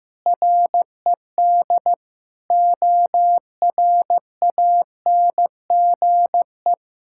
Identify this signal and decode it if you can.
Morse code, heard as a keyed tone, about 15 wpm, 700 Hz. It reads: RED ORANGE
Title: Pangram Challenge v2 - 15wpm